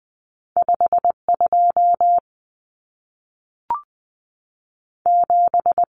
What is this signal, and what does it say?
Morse code, heard as a keyed tone, about 20 wpm, 700 Hz. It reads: 52 7
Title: Common Cut Numbers - 2-Digit 20wpm